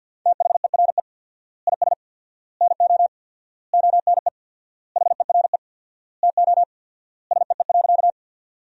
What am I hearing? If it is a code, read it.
Morse code, heard as a keyed tone, about 53 wpm, 700 Hz. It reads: THERE IS NO ONE HERE TO SEE.
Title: 4x Word-Spacing - Sentences from Top 200 Words - 50wpm